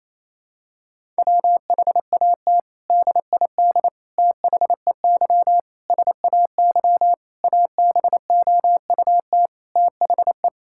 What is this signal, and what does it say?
Morse code, heard as a keyed tone, about 28 wpm, 700 Hz. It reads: WHAT DID THEY SAY ABOUT THE
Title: Sentences from Top 1000 Words - Review Part 5 - 28wpm